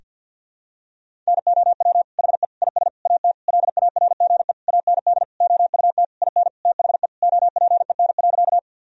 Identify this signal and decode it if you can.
Morse code, heard as a keyed tone, about 52 wpm, 700 Hz. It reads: NOW HE IS AT LARGE AND OUT IN THE OPEN.
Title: Sentences from Top 1000 Words - Review Part 1 - 50wpm